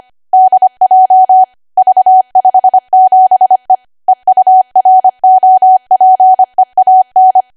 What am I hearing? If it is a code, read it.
Morse code, heard as a keyed tone, about 25 wpm, 750 Hz. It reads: DJ V57E EUROPEAN
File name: Archive / 25 WPM / 25WPM